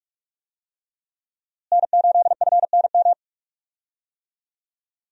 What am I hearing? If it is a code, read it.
Morse code, heard as a keyed tone, about 46 wpm, 700 Hz. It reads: N8FNM
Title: USA Call Signs - 5 Characters - Repeated 3x - 45wpm